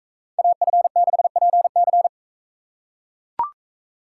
Morse code, heard as a keyed tone, about 41 wpm, 700 Hz. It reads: AF6PC